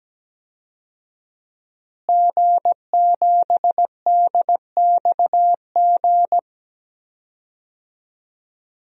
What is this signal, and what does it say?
Morse code, heard as a keyed tone, about 17 wpm, 700 Hz. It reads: G7DXG
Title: DXCC-100 Call Signs - 5 Characters - Repeated 3x - 17wpm